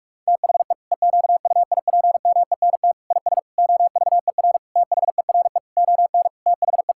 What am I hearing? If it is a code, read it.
Morse code, heard as a keyed tone, about 41 wpm, 700 Hz. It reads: THE EQUIPMENT IS OVER THERE ON THE
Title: Sentences from Top 1000 Words - Review Part 5 - 45wpm